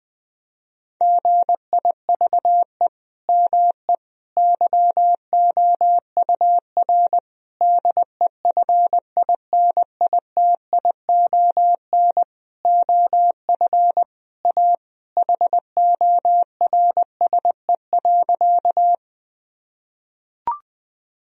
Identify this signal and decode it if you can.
Morse code, heard as a keyed tone, about 20 wpm, 700 Hz. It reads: GIVE ME YOUR DEFINITION OF A HORSE.
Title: Sentences from Top 1000 Words - Review Part 4 - 20wpm